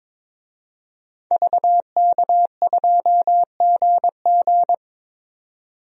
Morse code, heard as a keyed tone, about 22 wpm, 700 Hz. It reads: VK2GG
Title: DXCC-100 Call Signs - 5 Characters - 22wpm